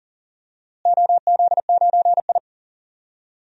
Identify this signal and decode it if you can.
Morse code, heard as a keyed tone, about 40 wpm, 700 Hz. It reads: OZ9I
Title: DXCC-100 Call Signs - 4 Characters - Repeated 2x - 40wpm